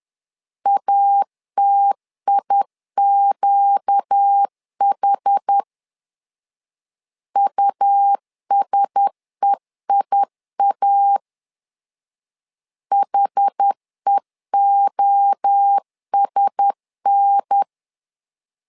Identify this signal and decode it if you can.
Morse code, heard as a keyed tone, about 10 wpm, 800 Hz. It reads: ATIQH USEIA HEOSN
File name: LEZIONI / 4 Q1S 10wpm 2133